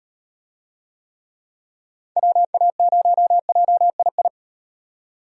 Morse code, heard as a keyed tone, about 38 wpm, 700 Hz. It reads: WA0JII